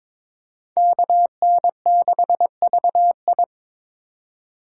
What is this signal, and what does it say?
Morse code, heard as a keyed tone, about 25 wpm, 700 Hz. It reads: KN6VI
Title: Speed Racing - USA 2x2 Call Signs 22wpm